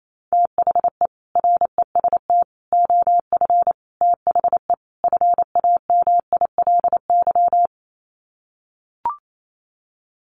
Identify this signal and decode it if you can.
Morse code, heard as a keyed tone, about 28 wpm, 700 Hz. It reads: THE REST OF THE FAMILY
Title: Sets of 5 Words Encore2 28wpm